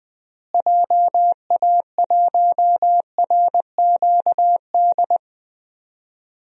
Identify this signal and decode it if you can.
Morse code, heard as a keyed tone, about 20 wpm, 700 Hz. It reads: JA1RQD
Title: DXCC-100 Call Signs - 6 Characters - Repeated 3x - 20wpm